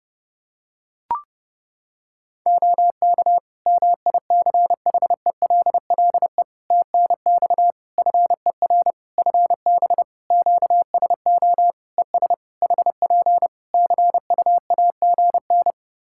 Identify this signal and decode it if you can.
Morse code, heard as a keyed tone, about 30 wpm, 700 Hz. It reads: OK MICHELLE TNX FER FB QSO ES HP CUAGN
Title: Practice QSO - Exchange 4 - Ending - 30wpm